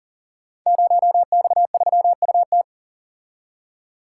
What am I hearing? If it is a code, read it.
Morse code, heard as a keyed tone, about 38 wpm, 700 Hz. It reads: 0X3UT
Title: Code Groups - 5 Letter:Numbers - 40wpm